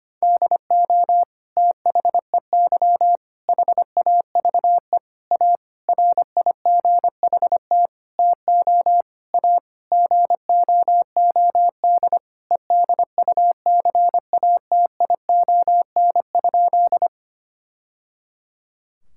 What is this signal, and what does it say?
Morse code, heard as a keyed tone, about 25 wpm, 700 Hz. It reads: DO THEY HAVE A RIGHT TO A GOOD EDUCATION?
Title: Sentences from Top 1000 Words - Review Part 6 - 25wpm